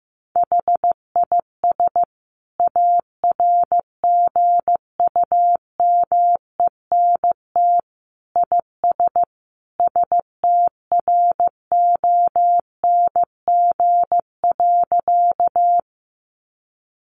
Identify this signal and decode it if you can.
Morse code, heard as a keyed tone, about 15 wpm, 700 Hz. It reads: HIS ARGUMENT IS STRONG.
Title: Sentences from Top 1400 Words 15wpm